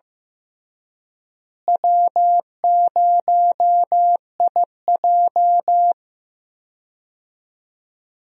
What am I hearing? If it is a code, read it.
Morse code, heard as a keyed tone, about 15 wpm, 700 Hz. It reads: W0IJ
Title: USA Call Signs - 4 Characters - Repeated 2x - 15wpm